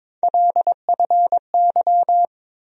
Morse code, heard as a keyed tone, about 22 wpm, 700 Hz. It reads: LFY
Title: USA Call Sign Postfixes - 3 Letters 22wpm